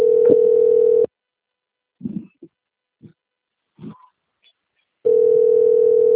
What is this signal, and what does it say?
Ringback tone (North America), 440 + 480 Hz.
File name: out- - - - - .51021